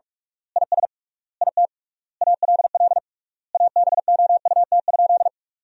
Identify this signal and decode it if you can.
Morse code, heard as a keyed tone, about 44 wpm, 700 Hz. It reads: IS IT ALL ABOUT?
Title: 3x Word-Spacing - Sentences from Top 100 Words - 45wpm